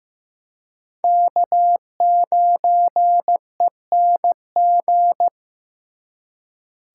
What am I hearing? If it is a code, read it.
Morse code, heard as a keyed tone, about 18 wpm, 700 Hz. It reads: K9ENG
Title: Speed Racing - USA 1x3 Call Signs 15wpm